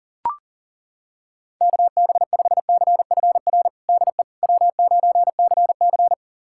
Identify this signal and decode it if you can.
Morse code, heard as a keyed tone, about 40 wpm, 700 Hz. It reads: KB5CFR DE W9CC
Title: USA Callsign DE Callsign 40wpm